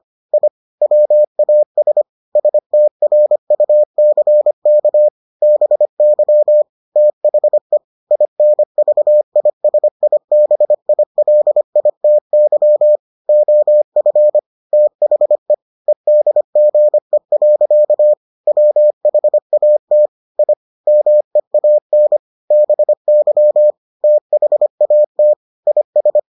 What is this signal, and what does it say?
Morse code, heard as a keyed tone, about 25 wpm, 600 Hz. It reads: I WAS STRUCK BY THE INVISIBILITY OF THE EDGE. WHAT I MEAN BY THAT IS